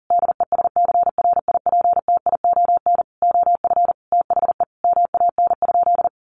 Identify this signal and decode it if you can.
Morse code, heard as a keyed tone, about 37 wpm, 700 Hz. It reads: DESCRIPTION OF THE MAN?
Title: Sentences from Top 1300 Words 40wpm